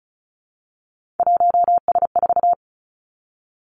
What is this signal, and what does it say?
Morse code, heard as a keyed tone, about 36 wpm, 700 Hz. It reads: 1S4
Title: Three Challenging Letter-Number Combinations 35wpm